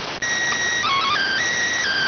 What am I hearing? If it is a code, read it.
SSTV (slow-scan television), PD 290.